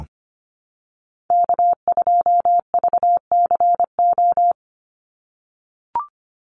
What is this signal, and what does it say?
Morse code, heard as a keyed tone, about 25 wpm, 700 Hz. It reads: K2VCO
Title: Call Signs - 25wpm